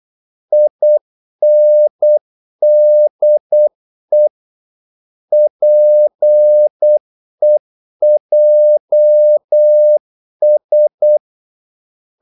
Morse code, heard as a keyed tone, about 8 wpm, 600 Hz. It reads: INDE PEJS
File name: Korte ord DK 8wpm